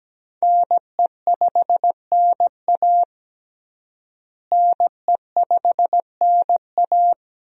Morse code, heard as a keyed tone, about 17 wpm, 700 Hz. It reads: NE5NA NE5NA
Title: USA Call Signs - 5 Characters - Repeated 2x - 17wpm